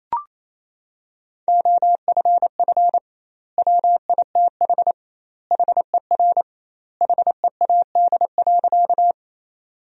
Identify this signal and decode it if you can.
Morse code, heard as a keyed tone, about 28 wpm, 700 Hz. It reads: OFF WITH HER HEAD.
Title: 2x Word-Spacing - Sentences from Top 300 Words - 28wpm